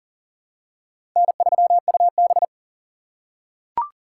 Morse code, heard as a keyed tone, about 39 wpm, 700 Hz. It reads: N3UB
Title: USA Call Signs - 4 Characters - 3x Character Spacing - 40wpm